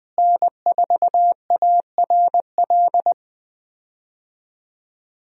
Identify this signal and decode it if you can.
Morse code, heard as a keyed tone, about 20 wpm, 700 Hz. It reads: N4ARL
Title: USA Call Signs - 5 Characters - Repeated 2x - 20wpm